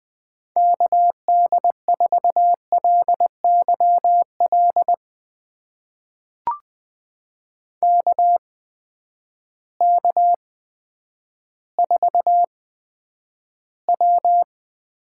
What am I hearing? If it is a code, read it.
Morse code, heard as a keyed tone, about 20 wpm, spaced as for 10 wpm, 700 Hz. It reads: KD4LYL KK4W
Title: USA Call Signs - 8x Character Spacing - 20wpm